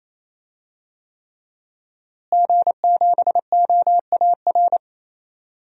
Morse code, heard as a keyed tone, about 28 wpm, 700 Hz. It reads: G7OAR
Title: DXCC-100 Call Signs - 5 Characters - Repeated 3x - 28wpm